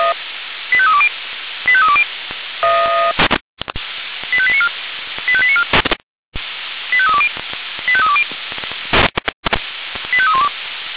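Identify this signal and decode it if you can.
Five-tone selective call (ZVEI), heard as quick runs of five tones, tones of about 70 ms. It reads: 85320 85320 85894 85894 85320 85320 85312
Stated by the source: ZVEI five-tone selective call, code 85894, received by SDR.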